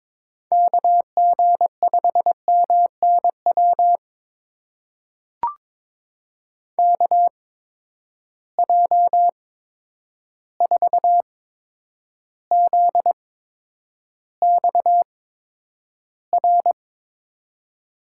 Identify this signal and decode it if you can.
Morse code, heard as a keyed tone, about 21 wpm, spaced as for 9 wpm, 700 Hz. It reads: KG5MNW KJ4ZXR
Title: USA Call Signs - 6 Characters - 8x Character Spacing - 22wpm